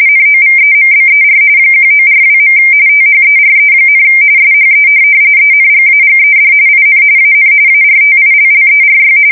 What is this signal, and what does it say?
RTTY, about 45.45 baud. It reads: NOVEMBER 1942. THERE, SHE ENGAGED IN AN ARTILLERY